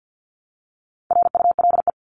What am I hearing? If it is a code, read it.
Morse code, heard as a keyed tone, about 50 wpm, 700 Hz. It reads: RULE
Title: Top 500 Words Repeated 50wpm